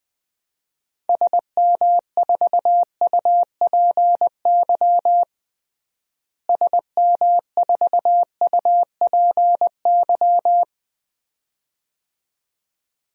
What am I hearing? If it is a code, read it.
Morse code, heard as a keyed tone, about 20 wpm, 700 Hz. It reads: SM4UPY SM4UPY
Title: DXCC-100 Call Signs - 6 Characters - Repeated 2x - 20wpm